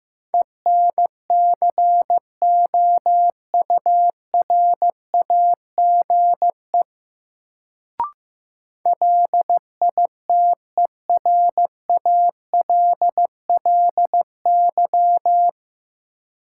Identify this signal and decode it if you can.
Morse code, heard as a keyed tone, about 15 wpm, 700 Hz. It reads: ENCOURAGE LITERALLY
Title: Sentences from Top 1400 Words - New Word Drill Alt - 15wpm